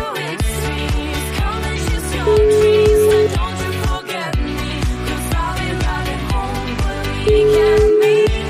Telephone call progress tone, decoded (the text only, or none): ringback tone